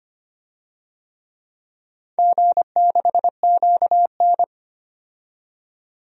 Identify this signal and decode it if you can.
Morse code, heard as a keyed tone, about 25 wpm, 700 Hz. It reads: G6QN
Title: DXCC-100 Call Signs - 4 Characters - 25wpm